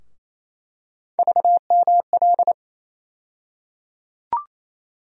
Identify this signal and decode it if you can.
Morse code, heard as a keyed tone, about 28 wpm, 700 Hz. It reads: VML
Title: USA Call Sign Postfixes - 3 Letters 28wpm